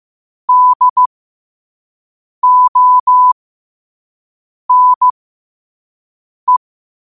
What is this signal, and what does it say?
Morse code, heard as a keyed tone, about 15 wpm, spaced as for 5 wpm, 1000 Hz. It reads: DONE
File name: qotdfiles / 5wpm